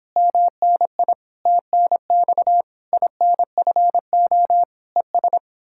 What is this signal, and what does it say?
Morse code, heard as a keyed tone, about 31 wpm, 700 Hz. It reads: MNI TNX INFO ES